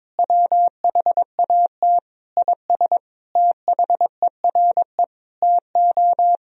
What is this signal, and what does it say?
Morse code, heard as a keyed tone, about 22 wpm, 700 Hz. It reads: WHAT IS THERE TO
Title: Sentences from Top 200 Words and Abbreviations 22wpm